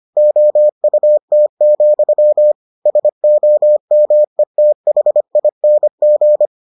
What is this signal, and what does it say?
Morse code, heard as a keyed tone, about 25 wpm, 600 Hz. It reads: OUT, SOMETHING